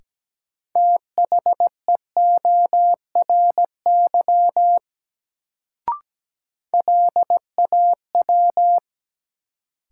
Morse code, heard as a keyed tone, about 17 wpm, 700 Hz. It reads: THEORY LAW
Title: Sentences from Top 600 Words - New Word Drill - 17wpm